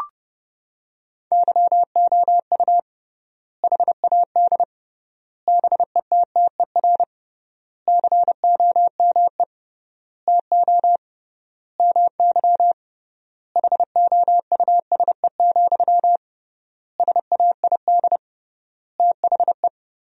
Morse code, heard as a keyed tone, about 32 wpm, 700 Hz. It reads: YOU HAD BETTER COME TO MY HOUSE, SAID THE
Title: 3x Word-Spacing - Sentences from Top 400 Words - 30wpm